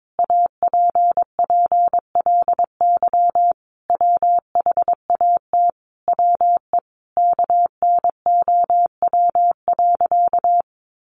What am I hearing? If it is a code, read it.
Morse code, heard as a keyed tone, about 22 wpm, 700 Hz. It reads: APPLY WHAT WE KNOW.
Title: Sentences from Top 1200 Words 22wpm